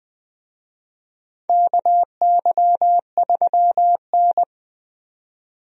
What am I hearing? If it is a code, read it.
Morse code, heard as a keyed tone, about 21 wpm, 700 Hz. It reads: KY3N
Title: USA Call Signs - 4 Characters - 20wpm